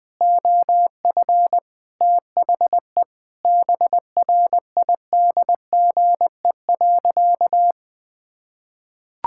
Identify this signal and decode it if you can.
Morse code, heard as a keyed tone, about 20 wpm, 700 Hz. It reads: OF THE BRIDGE.
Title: Sentences from Top 1600 Words 20wpm